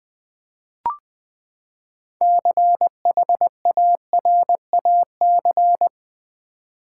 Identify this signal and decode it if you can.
Morse code, heard as a keyed tone, about 20 wpm, 700 Hz. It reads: CHARAC
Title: Build A Word - 6 Letters At A Time - 12-Letter Words - 20wpm